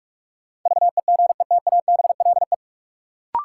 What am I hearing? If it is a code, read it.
Morse code, heard as a keyed tone, about 46 wpm, 700 Hz. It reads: VEGETABLE
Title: Sentences from Top 1300 Words - New Word Drill - 45wpm